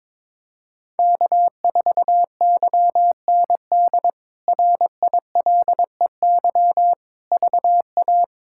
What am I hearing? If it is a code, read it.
Morse code, heard as a keyed tone, about 21 wpm, 700 Hz. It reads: K4YND RILEY VA
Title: K1USN SST QSO Practice - 22wpm